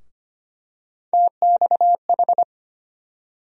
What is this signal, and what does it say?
Morse code, heard as a keyed tone, about 25 wpm, 700 Hz. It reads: TXH
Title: USA Call Sign Postfixes - 3 Letters 25wpm